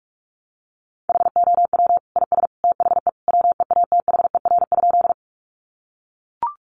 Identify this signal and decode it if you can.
Morse code, heard as a keyed tone, about 44 wpm, 700 Hz. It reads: HOW IS THE WEATHER?
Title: Sentences from Top 800 Words 45wpm